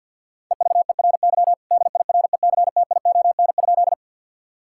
Morse code, heard as a keyed tone, about 51 wpm, 700 Hz. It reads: EVERY DIRECTION?
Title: Sentences from Top 1000 Words - Review Part 5 - 50wpm